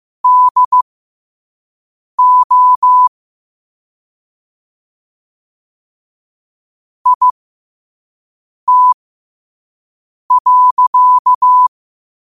Morse code, heard as a keyed tone, about 15 wpm, spaced as for 5 wpm, 1000 Hz. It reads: DO IT.
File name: qotdfiles / 5wpm